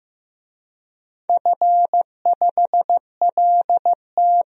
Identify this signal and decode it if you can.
Morse code, heard as a keyed tone, about 15 wpm, 700 Hz. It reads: F5LT
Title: DXCC-100 Call Signs - 4 Characters - 15wpm